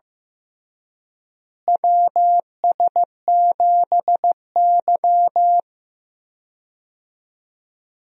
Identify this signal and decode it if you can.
Morse code, heard as a keyed tone, about 15 wpm, 700 Hz. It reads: WS7Y
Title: USA Call Signs - 4 Characters - Repeated 2x - 15wpm